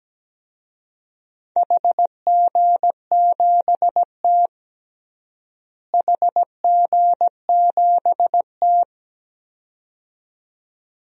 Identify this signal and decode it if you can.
Morse code, heard as a keyed tone, about 17 wpm, 700 Hz. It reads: HG7T HG7T
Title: DXCC-100 Call Signs - 4 Characters - Repeated 2x - 17wpm